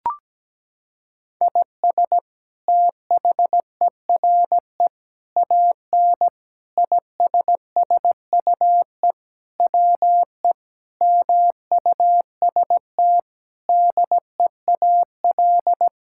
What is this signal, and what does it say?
Morse code, heard as a keyed tone, about 17 wpm, 700 Hz. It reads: IS THERE AN ISSUE WE MUST DEAL
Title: Sentences from Top 1000 Words - Review Part 6 - 17wpm